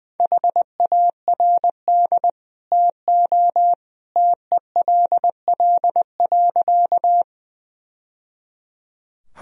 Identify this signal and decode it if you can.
Morse code, heard as a keyed tone, about 21 wpm, 700 Hz. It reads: HARD TO TELL.
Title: Sentences from Top 1000 Words - Review Part 1 - 20wpm